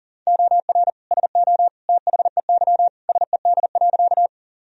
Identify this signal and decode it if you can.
Morse code, heard as a keyed tone, about 40 wpm, 700 Hz. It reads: OR SO THEY SED.
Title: Sentences from Top 100 Words and Abbreviations 40wpm